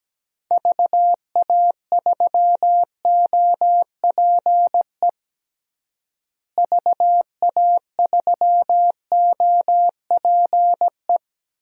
Morse code, heard as a keyed tone, about 17 wpm, 700 Hz. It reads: VA3OPE VA3OPE
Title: DXCC-100 Call Signs - 6 Characters - Repeated 2x - 17wpm